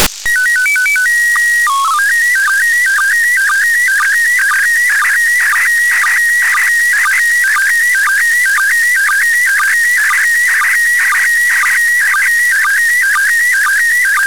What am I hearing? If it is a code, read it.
SSTV (slow-scan television), PD 120.
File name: samples ogg / satnogs